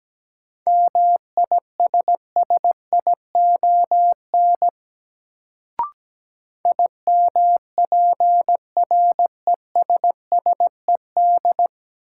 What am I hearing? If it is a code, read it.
Morse code, heard as a keyed tone, about 17 wpm, 700 Hz. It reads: MISSION IMPRESSED
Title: Sentences from Top 1600 Words - New Word Drill Alt - 17wpm